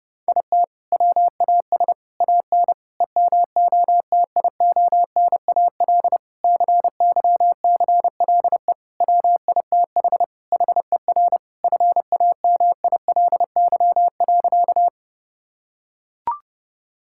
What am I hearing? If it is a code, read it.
Morse code, heard as a keyed tone, about 30 wpm, 700 Hz. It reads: IT WAS AN EMOTIONAL CYCLE WITH HER FAMILY.
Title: Sentences from Top 1000 Words - Review Part 6 - 30wpm